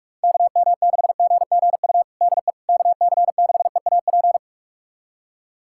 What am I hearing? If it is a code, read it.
Morse code, heard as a keyed tone, about 46 wpm, 700 Hz. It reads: KM6GGU DE KC6EAP
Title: USA Callsign DE Callsign 45wpm